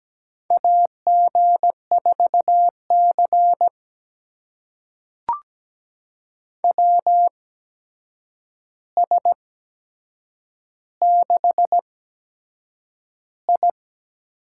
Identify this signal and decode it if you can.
Morse code, heard as a keyed tone, about 17 wpm, spaced as for 7 wpm, 700 Hz. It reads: AG4C WS6I
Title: USA Call Signs - 4 Characters - 8x Character Spacing - 17wpm